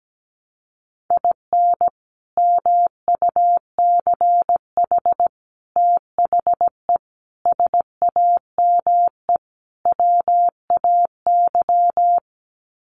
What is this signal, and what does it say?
Morse code, heard as a keyed tone, about 17 wpm, 690 Hz. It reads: IN MUCH THE SAME WAY
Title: Sets of 5 Words Encore 17wpm